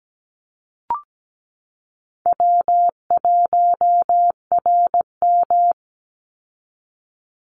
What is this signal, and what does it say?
Morse code, heard as a keyed tone, about 17 wpm, 700 Hz. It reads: W1RM
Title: Easy Call Signs Repeated 3x - 17wpm